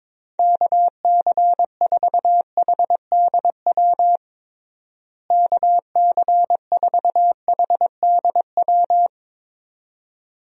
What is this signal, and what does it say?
Morse code, heard as a keyed tone, about 21 wpm, 700 Hz. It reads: KC4HDW KC4HDW
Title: USA Call Signs - 6 Characters - Repeated 2x - 22wpm